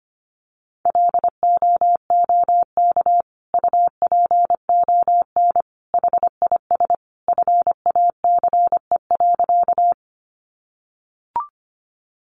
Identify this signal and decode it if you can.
Morse code, heard as a keyed tone, about 25 wpm, 700 Hz. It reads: LOOK UPON HIS FACE.
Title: Sentences from Top 1100 Words 25wpm